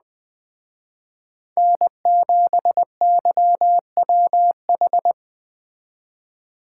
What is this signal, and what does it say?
Morse code, heard as a keyed tone, about 22 wpm, 700 Hz. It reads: N7YWH